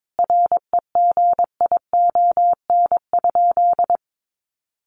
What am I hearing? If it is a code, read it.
Morse code, heard as a keyed tone, about 22 wpm, 700 Hz. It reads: REGION?
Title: Sentences from Top 1500 Words 22wpm